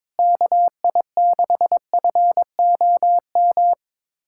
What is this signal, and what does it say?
Morse code, heard as a keyed tone, about 22 wpm, 700 Hz. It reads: KI6FOM
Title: USA Callsign DE Callsign - 6 Characters - 22wpm